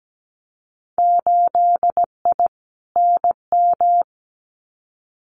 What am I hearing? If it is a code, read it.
Morse code, heard as a keyed tone, about 17 wpm, 700 Hz. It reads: 8I NM
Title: Winter Field Day Exchanges Encore 17wpm